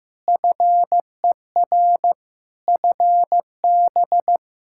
Morse code, heard as a keyed tone, about 15 wpm, 700 Hz. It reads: FER FB
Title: Practice QSO - Exchange 4 - Ending - 15wpm